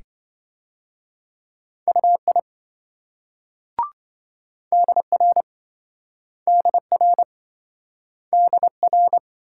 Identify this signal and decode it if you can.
Morse code, heard as a keyed tone, about 27 wpm, 700 Hz. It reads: UI DR DR DR